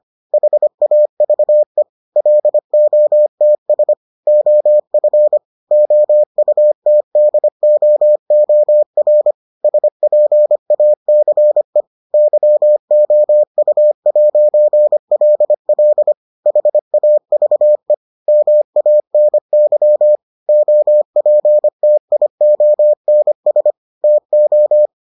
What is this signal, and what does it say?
Morse code, heard as a keyed tone, about 25 wpm, 600 Hz. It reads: HAVE LOTS OF OUTDOOR SPACE YOU'LL HAVE MANY OPTIONS TO